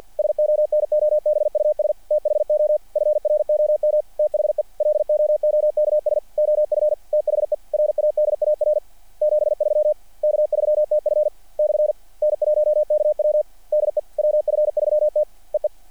Morse code, heard as a keyed tone, about 48 wpm, 600 Hz. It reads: SONOBUS THO FROM THE LOOKS OF THE RADAR 73 K3TF / N1KW DE WF3T EE
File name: archived / CRSnet